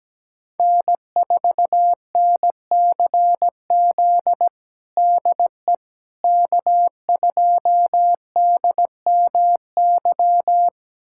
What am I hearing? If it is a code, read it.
Morse code, heard as a keyed tone, about 17 wpm, 700 Hz. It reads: N4NCZ DE K2DMY
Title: USA Callsign DE Callsign - 5 Characters - 17wpm